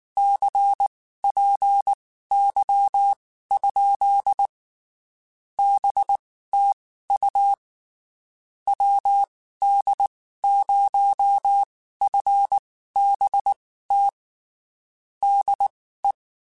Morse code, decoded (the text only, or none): CPY? BTU WD0FBT DE